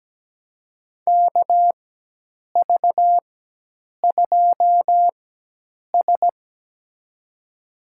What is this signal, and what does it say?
Morse code, heard as a keyed tone, about 17 wpm, spaced as for 8 wpm, 700 Hz. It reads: KV2S
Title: USA Call Signs - 4 Characters - 4x Character Spacing - 17wpm